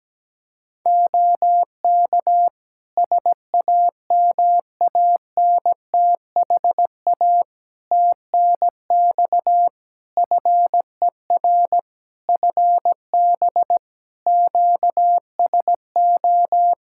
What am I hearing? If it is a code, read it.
Morse code, heard as a keyed tone, about 17 wpm, 700 Hz. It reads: OK SAMANTHA TNX FER FB QSO
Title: Practice QSO - Exchange 4 - Ending - 17wpm